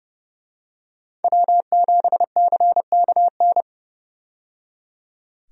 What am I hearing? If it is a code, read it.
Morse code, heard as a keyed tone, about 30 wpm, 700 Hz. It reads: W7CKN
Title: USA Call Signs - 5 Characters - 30wpm